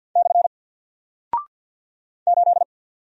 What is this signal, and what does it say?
Morse code, heard as a keyed tone, about 50 wpm, 700 Hz. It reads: / 8
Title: Single Character (Letter:Number:Punctuation:Prosign) 50wpm